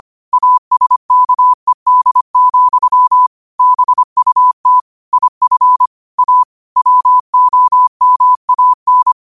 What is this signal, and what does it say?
Morse code, heard as a keyed tone, about 25 wpm, 1000 Hz. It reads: ASKED, BUT IF A WOMAN